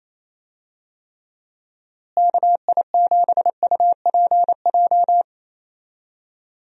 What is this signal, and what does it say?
Morse code, heard as a keyed tone, about 28 wpm, 700 Hz. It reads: KI7UPJ
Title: USA Call Signs - 6 Characters - Repeated 2x - 28wpm